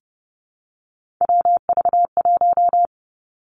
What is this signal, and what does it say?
Morse code, heard as a keyed tone, about 30 wpm, 700 Hz. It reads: WV1
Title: Three Challenging Letter-Number Combinations 30wpm